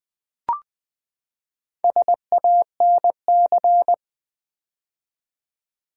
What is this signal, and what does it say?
Morse code, heard as a keyed tone, about 20 wpm, 700 Hz. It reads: SANC
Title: Build A Word - 4 Letters At A Time - 8-Letter Words - 20wpm